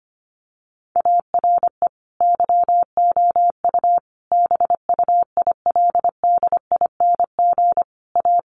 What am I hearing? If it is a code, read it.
Morse code, heard as a keyed tone, about 25 wpm, 700 Hz. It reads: ARE YOU BUILDING A
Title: Sentences from Top 700 Words 25wpm